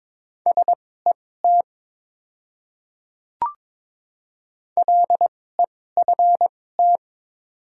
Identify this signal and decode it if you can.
Morse code, heard as a keyed tone, about 22 wpm, spaced as for 16 wpm, 700 Hz. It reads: SET LEFT
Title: 2x Character Spacing - Top 300 Words - 22wpm